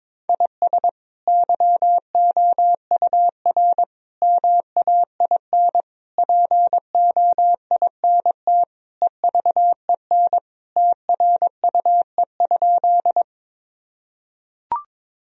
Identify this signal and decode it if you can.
Morse code, heard as a keyed tone, about 22 wpm, 700 Hz. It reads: IS YOUR MAIN POINT EVEN TRUE?
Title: Sentences from Top 1000 Words - Review Part 2 - 22wpm